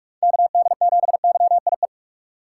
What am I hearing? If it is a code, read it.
Morse code, heard as a keyed tone, about 43 wpm, 700 Hz. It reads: KD7YIE
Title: USA Call Signs - 2x Character Spacing - 45wpm